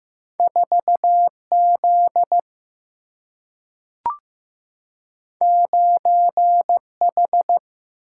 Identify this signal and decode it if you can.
Morse code, heard as a keyed tone, about 15 wpm, 700 Hz. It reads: 4Z 9H
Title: Single-Letter Single-Number 15wpm